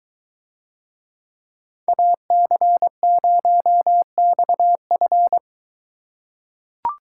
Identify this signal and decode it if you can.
Morse code, heard as a keyed tone, about 21 wpm, 700 Hz. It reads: AC0XF